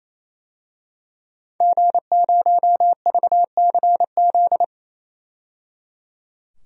Morse code, heard as a keyed tone, about 28 wpm, 700 Hz. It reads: G0VCZ
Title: DXCC-100 Call Signs - 5 Characters - 28wpm